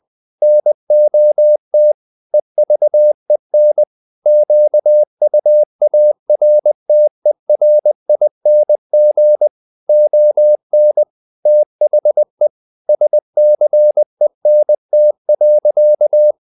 Morse code, heard as a keyed tone, about 20 wpm, 600 Hz. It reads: NOT EVEN QUARTERING ON THE SCENT.
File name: the old man and the sea part 2 20wpm